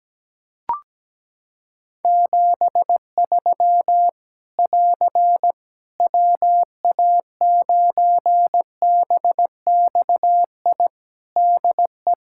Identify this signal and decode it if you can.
Morse code, heard as a keyed tone, about 17 wpm, 700 Hz. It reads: 73 <AR> WA9BXI DE
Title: K1USN SST QSO Practice 17wpm